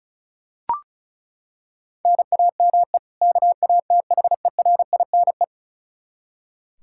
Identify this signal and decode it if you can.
Morse code, heard as a keyed tone, about 36 wpm, 700 Hz. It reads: NAME KATHERINE
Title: Practice QSO - Exchange 1 - Beginning - 35wpm